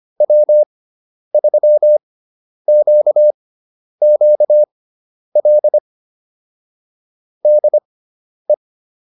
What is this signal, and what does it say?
Morse code, heard as a keyed tone, about 25 wpm, spaced as for 10 wpm, 600 Hz. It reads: W3QQL DE
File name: QSO309 10